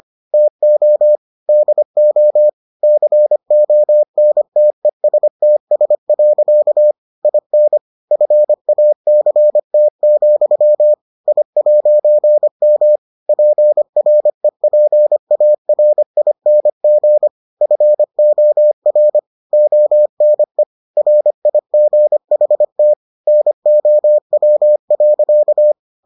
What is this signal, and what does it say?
Morse code, heard as a keyed tone, about 25 wpm, 600 Hz. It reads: TO DO CONTESTS. IN FACT, I'M PREPARING FOR ONE RIGHT NOW.